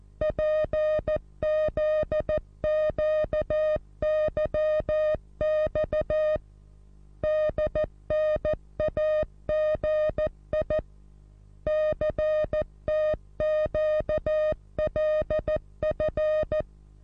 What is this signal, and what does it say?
Morse code, heard as a keyed tone, about 14 wpm, 620 Hz. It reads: PZQYX DNAGI CTQLF
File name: repos / Morsekurs / DJ4UF Morsen27